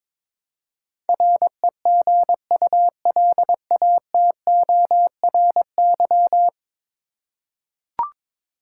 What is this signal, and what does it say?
Morse code, heard as a keyed tone, about 22 wpm, 700 Hz. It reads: REGULATORY
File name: practice / 22wpm / 10 Letter Words - First 3 Letters Given - 22wpm